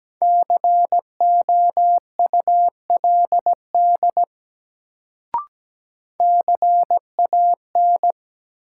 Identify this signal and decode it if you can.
Morse code, heard as a keyed tone, about 17 wpm, 700 Hz. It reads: COULD CAN
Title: Sentences from Top 100 Words - New Word Drill - 17wpm